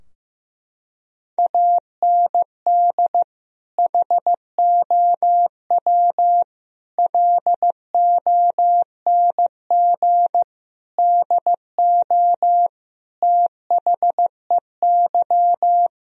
Morse code, heard as a keyed tone, about 16 wpm, 700 Hz. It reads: AND HOW LONG DO THEY
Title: Sentences from Top 1000 Words - Review Part 4 - 15wpm